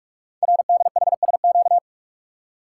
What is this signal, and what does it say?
Morse code, heard as a keyed tone, about 44 wpm, 690 Hz. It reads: RDHSQ
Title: Code Groups - 5 Letters - 45wpm